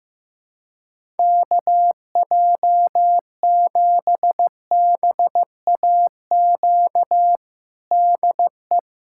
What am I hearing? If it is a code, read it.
Morse code, heard as a keyed tone, about 15 wpm, 700 Hz. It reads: KJ7BAQ DE
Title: USA Callsign DE Callsign 15wpm